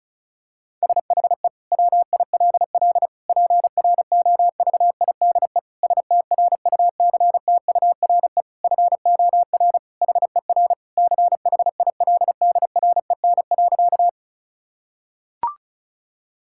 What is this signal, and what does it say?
Morse code, heard as a keyed tone, about 34 wpm, 700 Hz. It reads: SHE WILL PROVIDE STRUCTURE FOR HER CHILDREN.
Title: Sentences from Top 1000 Words - Review Part 6 - 35wpm